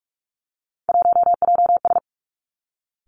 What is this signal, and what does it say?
Morse code, heard as a keyed tone, about 46 wpm, 700 Hz. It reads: 1JS
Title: Three Challenging Letter-Number Combinations 45wpm